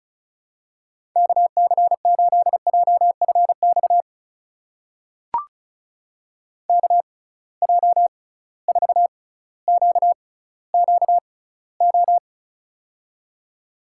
Morse code, heard as a keyed tone, about 36 wpm, spaced as for 18 wpm, 700 Hz. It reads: KC8JFX KJ4QQO
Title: USA Call Signs - 6 Characters - 6x Character Spacing - 35wpm